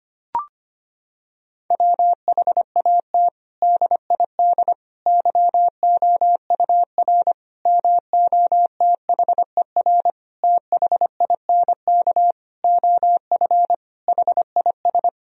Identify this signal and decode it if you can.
Morse code, heard as a keyed tone, about 25 wpm, 700 Hz. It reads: WHAT DID YOUR MOTHER THINK OF HIS
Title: Sentences from Top 1000 Words - Review Part 4 - 25wpm